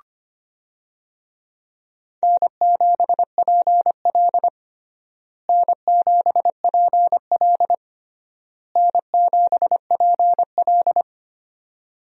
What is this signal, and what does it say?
Morse code, heard as a keyed tone, about 25 wpm, 700 Hz. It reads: N7PL N7PL N7PL
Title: USA Call Signs - 4 Characters - Repeated 3x - 25wpm